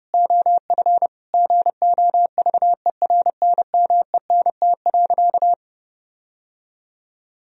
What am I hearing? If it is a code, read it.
Morse code, heard as a keyed tone, about 30 wpm, 700 Hz. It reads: OF GOVERNMENT.
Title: Sentences from Top 1000 Words - Review Part 3 - 30wpm